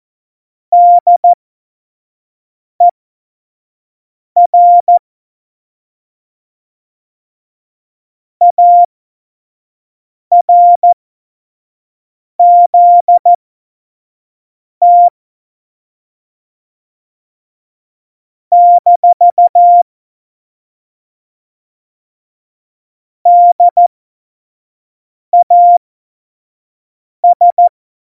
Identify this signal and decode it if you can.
Morse code, heard as a keyed tone, about 14 wpm, spaced as for 5 wpm, 700 Hz. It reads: DER ARZT - DAS
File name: Morsekurs / Witze / Witz 27 March 2018, 25 F70